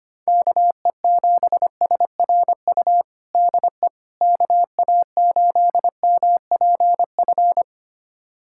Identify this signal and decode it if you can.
Morse code, heard as a keyed tone, about 25 wpm, 700 Hz. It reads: KE7SRU DE KA8MPF
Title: USA Callsign DE Callsign 25wpm